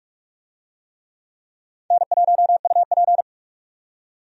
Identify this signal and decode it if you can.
Morse code, heard as a keyed tone, about 46 wpm, 700 Hz. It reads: N1UP
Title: USA Call Signs - 4 Characters - 45wpm